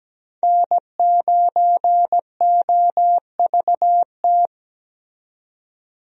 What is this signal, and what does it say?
Morse code, heard as a keyed tone, about 17 wpm, 700 Hz. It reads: N9OVT
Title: USA Call Signs - 5 Characters - 17wpm